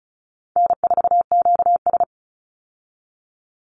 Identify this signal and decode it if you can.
Morse code, heard as a keyed tone, about 34 wpm, 700 Hz. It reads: N4QS
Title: Easy Call Signs Repeated 2x - 35wpm